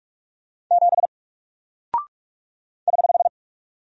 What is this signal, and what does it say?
Morse code, heard as a keyed tone, about 44 wpm, 700 Hz. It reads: 7 <HH>
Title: Single Character (Letter:Number:Punctuation:Prosign) Alt 45wpm